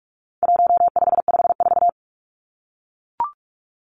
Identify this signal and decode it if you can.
Morse code, heard as a keyed tone, about 44 wpm, 700 Hz. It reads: 1554
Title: Numbers - 4-Digit 45wpm